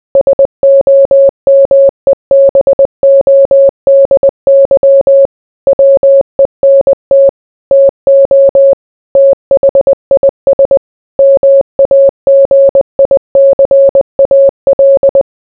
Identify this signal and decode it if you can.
Morse code, heard as a keyed tone, about 20 wpm, 550 Hz. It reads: SOMEBODY WENT TO THIS MAGICAL